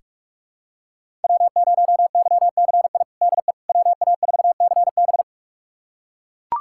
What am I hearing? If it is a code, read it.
Morse code, heard as a keyed tone, about 42 wpm, 700 Hz. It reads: W0YCI DE WA4CB
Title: USA Callsign DE Callsign - 5 Characters - 45wpm